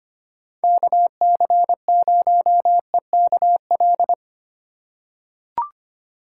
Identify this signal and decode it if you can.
Morse code, heard as a keyed tone, about 25 wpm, 700 Hz. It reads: KC0EKL
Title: USA Call Signs - 4x Character Spacing - 25wpm